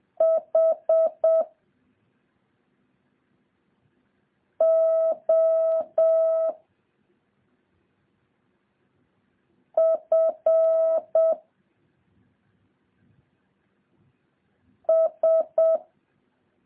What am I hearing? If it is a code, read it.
Morse code, heard as a keyed tone, about 7 wpm, 650 Hz. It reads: H O F S
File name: jh8cbh / 2023cw